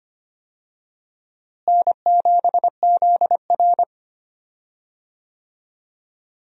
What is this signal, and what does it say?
Morse code, heard as a keyed tone, about 25 wpm, 700 Hz. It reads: N7ZR
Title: USA Call Signs - 4 Characters - Repeated 2x - 25wpm